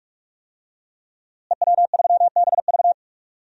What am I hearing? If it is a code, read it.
Morse code, heard as a keyed tone, about 42 wpm, 700 Hz. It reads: EW3BV
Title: DXCC-100 Call Signs - 5 Characters - Repeated 2x - 45wpm